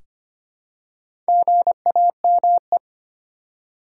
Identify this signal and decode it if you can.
Morse code, heard as a keyed tone, about 25 wpm, 690 Hz. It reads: GAME
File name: practice / 25wpm / Easy Words with Hints - 25wpm